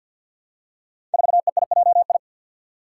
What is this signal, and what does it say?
Morse code, heard as a keyed tone, about 52 wpm, 700 Hz. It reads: 4EIJI
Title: Five Challenging Letter-Number Combinations 50wpm